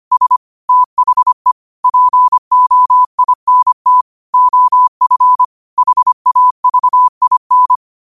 Morse code, heard as a keyed tone, about 25 wpm, 1000 Hz. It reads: S THE POINT OF HAVIN
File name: qotdfiles / 25wpm